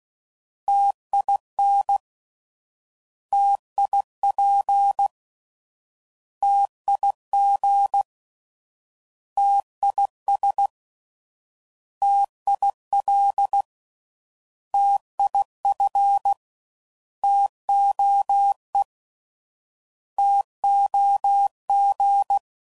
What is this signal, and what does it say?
Morse code, heard as a keyed tone, about 16 wpm, 800 Hz. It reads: TIN TIP TIG TIS TIL TIF TOE TOG